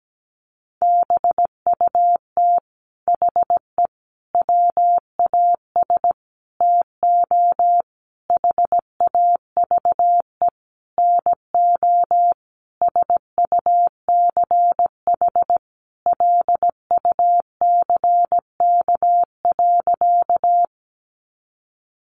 Morse code, heard as a keyed tone, about 17 wpm, 700 Hz. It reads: BUT HE WAS TO HAVE NO SUCH LUCK.
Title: Sentences from Top 1500 Words 17wpm